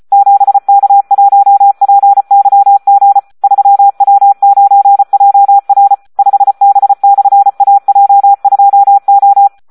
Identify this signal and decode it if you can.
Morse code, heard as a keyed tone, about 34 wpm, 800 Hz. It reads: 7K1PYG 3W9JR 5B/AJ2O
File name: Audio practica CW / SD 1 34